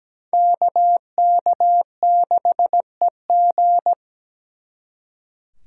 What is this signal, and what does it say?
Morse code, heard as a keyed tone, about 20 wpm, 700 Hz. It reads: KK6EG
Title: Speed Racing - USA 2x2 Call Signs 17wpm